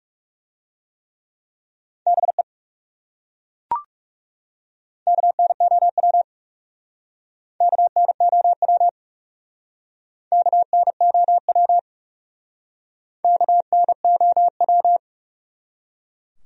Morse code, heard as a keyed tone, about 38 wpm, 700 Hz. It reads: BE KNOW KNOW KNOW KNOW
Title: Speed-Racing Top 200 Words 30wpm